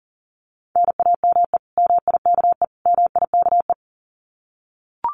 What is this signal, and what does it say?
Morse code, heard as a keyed tone, about 40 wpm, 700 Hz. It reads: NAME MIKE MIKE
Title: Common QSO Phrases Encore 40wpm